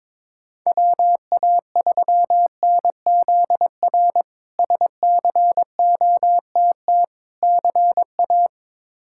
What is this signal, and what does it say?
Morse code, heard as a keyed tone, about 22 wpm, 700 Hz. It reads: WA3NZR SCOTT CA
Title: K1USN SST QSO Practice - 22wpm